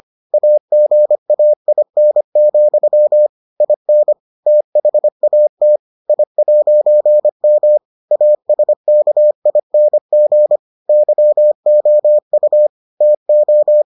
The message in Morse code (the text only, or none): AGAIN, IN THAT I'M ASKING YOU TO